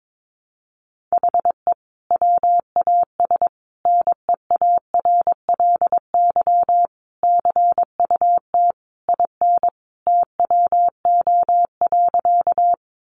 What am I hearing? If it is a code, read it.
Morse code, heard as a keyed tone, about 22 wpm, 700 Hz. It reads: HE WAS NEARLY CUT IN TWO.
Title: Sentences from Top 1300 Words 22wpm